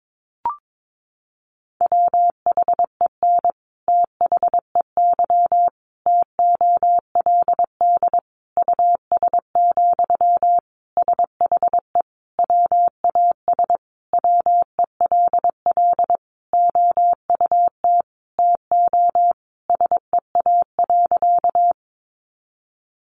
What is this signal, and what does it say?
Morse code, heard as a keyed tone, about 22 wpm, 700 Hz. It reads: WHEN THEY TOLD US, SHE WAS WELL OUT TO SEA.
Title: Sentences from Top 400 Words 22wpm